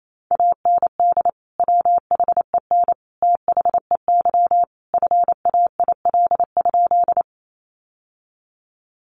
Morse code, heard as a keyed tone, about 28 wpm, 700 Hz. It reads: AND WHEN THEY FAIL?
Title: Sentences from Top 1100 Words 28wpm